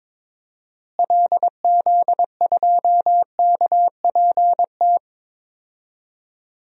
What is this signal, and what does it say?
Morse code, heard as a keyed tone, about 22 wpm, 700 Hz. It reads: LZ2KPT
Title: DXCC-100 Call Signs - 6 Characters - Repeated 3x - 22wpm